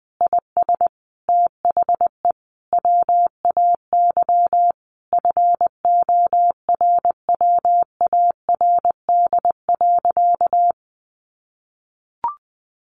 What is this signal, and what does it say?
Morse code, heard as a keyed tone, about 20 wpm, 700 Hz. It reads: IS THE WAY FORWARD.
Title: Sentences from Top 1500 Words 20wpm